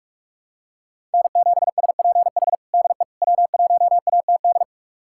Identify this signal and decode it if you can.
Morse code, heard as a keyed tone, about 42 wpm, 700 Hz. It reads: N7SPH DE W1ATD
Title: USA Callsign DE Callsign - 5 Characters - 45wpm